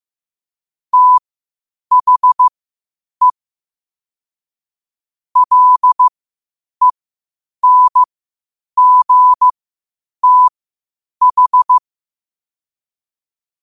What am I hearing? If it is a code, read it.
Morse code, heard as a keyed tone, about 15 wpm, spaced as for 8 wpm, 1000 Hz. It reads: THE LENGTH
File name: qotdfiles / 8wpm